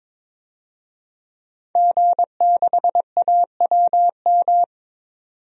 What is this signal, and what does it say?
Morse code, heard as a keyed tone, about 21 wpm, 700 Hz. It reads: G6AWM
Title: DXCC-100 Call Signs - 5 Characters - 22wpm